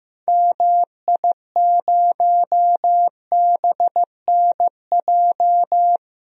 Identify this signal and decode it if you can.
Morse code, heard as a keyed tone, about 15 wpm, 700 Hz. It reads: MI0BNJ
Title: DXCC-100 Call Signs - 6 Characters - 15wpm